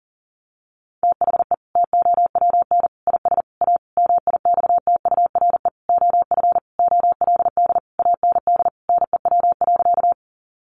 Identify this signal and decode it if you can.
Morse code, heard as a keyed tone, about 40 wpm, 700 Hz. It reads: THE TOWN IS A MIXTURE OF OLD AND NEW.
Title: Sentences from Top 1500 Words 40wpm